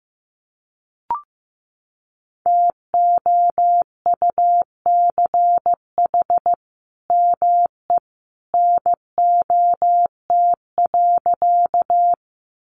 Morse code, heard as a keyed tone, about 15 wpm, 700 Hz. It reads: TOUCH ME NOT.
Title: Sentences from Top 1200 Words 15wpm